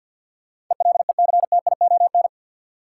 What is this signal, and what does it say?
Morse code, heard as a keyed tone, about 53 wpm, 700 Hz. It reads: ELECTION
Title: Sentences from Top 1500 Words - New Word Drill - 50wpm